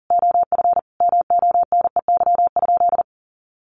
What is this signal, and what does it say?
Morse code, heard as a keyed tone, about 40 wpm, 700 Hz. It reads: OF MONEY?
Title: Sentences from Top 1100 Words 40wpm